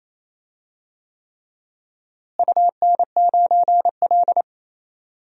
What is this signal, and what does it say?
Morse code, heard as a keyed tone, about 28 wpm, 700 Hz. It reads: UN9L
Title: DXCC-100 Call Signs - 4 Characters - Repeated 3x - 28wpm